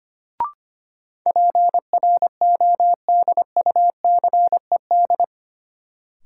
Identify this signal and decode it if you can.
Morse code, heard as a keyed tone, about 25 wpm, 700 Hz. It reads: PRODUCED
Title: Sentences from Top 1000 Words - Review Part 6 - New Word Drill - 25wpm